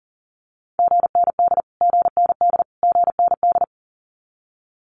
Morse code, heard as a keyed tone, about 40 wpm, 700 Hz. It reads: GND GND GND
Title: Abbreviations Repeated 3x - 40wpm